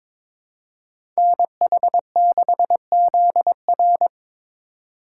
Morse code, heard as a keyed tone, about 22 wpm, 700 Hz. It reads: NH6ZR
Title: USA Call Signs - 5 Characters - 3x Character Spacing - 22wpm